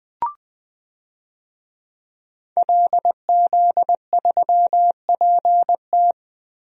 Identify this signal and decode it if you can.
Morse code, heard as a keyed tone, about 20 wpm, 700 Hz. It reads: LZ3PT
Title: DXCC-100 Call Signs - 5 Characters - Repeated 3x - 20wpm